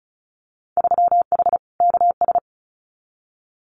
Morse code, heard as a keyed tone, about 36 wpm, 700 Hz. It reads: 3H KS
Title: Winter Field Day Exchanges 35wpm